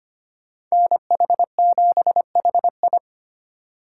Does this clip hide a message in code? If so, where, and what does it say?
Yes: Morse code from 0.7 to 3.0 s, heard as a keyed tone, about 32 wpm, 700 Hz. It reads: NH7HI